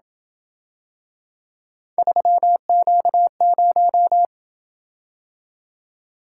Morse code, heard as a keyed tone, about 27 wpm, 700 Hz. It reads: 3Q0